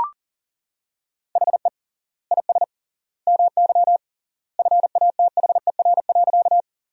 Morse code, heard as a keyed tone, about 37 wpm, 700 Hz. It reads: HE IS MY FATHER.
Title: 3x Word-Spacing - Sentences from Top 300 Words - 40wpm